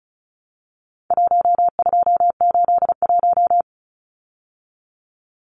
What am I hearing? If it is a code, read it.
Morse code, heard as a keyed tone, about 34 wpm, 700 Hz. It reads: 1281
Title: Numbers - 4-Digit 35wpm